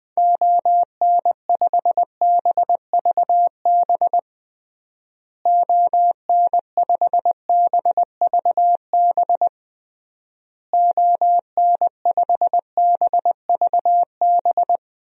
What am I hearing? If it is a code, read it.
Morse code, heard as a keyed tone, about 20 wpm, 700 Hz. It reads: ON5BVB ON5BVB ON5BVB
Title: DXCC-100 Call Signs - 6 Characters - Repeated 3x - 20wpm